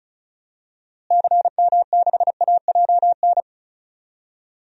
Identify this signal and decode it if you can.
Morse code, heard as a keyed tone, about 34 wpm, 700 Hz. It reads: CM6AJN
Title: DXCC-100 Call Signs - 6 Characters - Repeated 3x - 35wpm